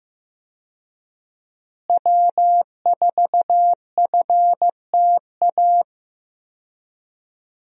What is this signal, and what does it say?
Morse code, heard as a keyed tone, about 15 wpm, 700 Hz. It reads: W4FTA
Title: USA Call Signs - 5 Characters - Repeated 2x - 15wpm